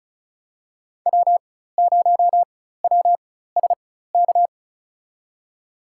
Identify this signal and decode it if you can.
Morse code, heard as a keyed tone, about 36 wpm, spaced as for 16 wpm, 700 Hz. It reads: W0WSK
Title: USA Call Signs - 4x Character Spacing - 35wpm